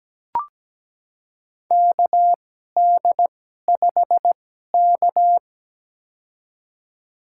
Morse code, heard as a keyed tone, about 17 wpm, spaced as for 12 wpm, 700 Hz. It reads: KD5K
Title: USA Call Signs - 2x Character Spacing - 17wpm